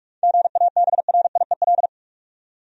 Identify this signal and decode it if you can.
Morse code, heard as a keyed tone, about 46 wpm, 700 Hz. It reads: GABRIEL
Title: USA Common Names - 45wpm